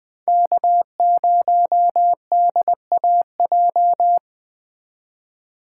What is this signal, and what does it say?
Morse code, heard as a keyed tone, about 21 wpm, 700 Hz. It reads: K0DAJ